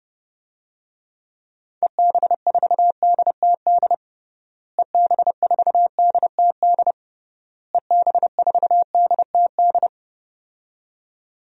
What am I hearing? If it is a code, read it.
Morse code, heard as a keyed tone, about 30 wpm, 700 Hz. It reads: EB4DTD EB4DTD EB4DTD
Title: DXCC-100 Call Signs - 6 Characters - Repeated 3x - 30wpm